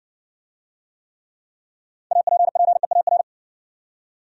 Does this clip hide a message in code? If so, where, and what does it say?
Yes: Morse code from 2.1 to 3.2 s, heard as a keyed tone, about 58 wpm, 700 Hz. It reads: APPEAR